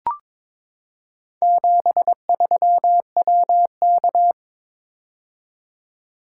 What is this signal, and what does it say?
Morse code, heard as a keyed tone, about 22 wpm, 700 Hz. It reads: 73WK
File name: practice / 22wpm / Four Letter-Number Combinations - 22wpm